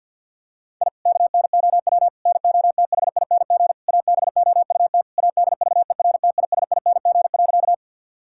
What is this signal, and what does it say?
Morse code, heard as a keyed tone, about 46 wpm, 700 Hz. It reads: I KNOW NOTHING ABOUT ADVERTISING.
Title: Sentences from Top 1000 Words - Review Part 6 - 50wpm